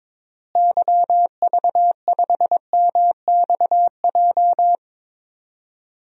Morse code, heard as a keyed tone, about 21 wpm, 700 Hz. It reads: YV5MXJ
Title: DXCC-100 Call Signs - 6 Characters - Repeated 2x - 22wpm